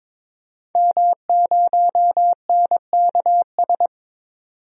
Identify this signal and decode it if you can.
Morse code, heard as a keyed tone, about 22 wpm, 700 Hz. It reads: M0NKS
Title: DXCC-100 Call Signs - 5 Characters - 22wpm